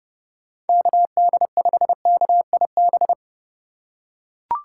Morse code, heard as a keyed tone, about 30 wpm, 700 Hz. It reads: KD5KIB
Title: USA Call Signs - 6 Characters - 4x Character Spacing - 30wpm